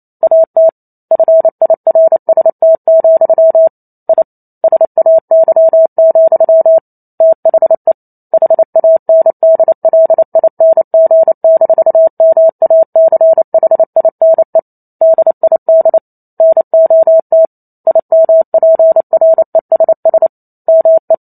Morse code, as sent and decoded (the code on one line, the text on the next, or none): .- - / ..-. .. .-. ... - --..-- / .. / ... .- -.-- --..-- / - .... / .... .- -. -.. .-.. .. -. --. -....- -- .- -.-. .... .. -. / -.. .. -.. / -. --- - / .. -- .--. .-. ... ... / -- .
AT FIRST, I SAY, THE HANDLING-MACHINE DID NOT IMPRESS ME